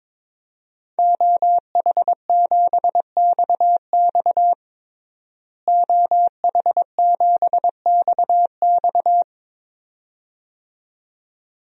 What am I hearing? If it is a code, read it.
Morse code, heard as a keyed tone, about 22 wpm, 700 Hz. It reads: OH7XX OH7XX
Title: DXCC-100 Call Signs - 5 Characters - Repeated 2x - 22wpm